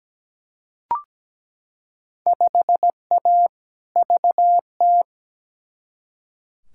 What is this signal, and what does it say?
Morse code, heard as a keyed tone, about 17 wpm, 700 Hz. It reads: 5A VT
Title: Field Day Exchanges Encore - 17wpm